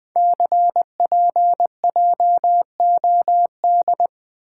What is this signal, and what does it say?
Morse code, heard as a keyed tone, about 20 wpm, 700 Hz. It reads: CPJOD
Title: Code Groups - 5 Letters - 20wpm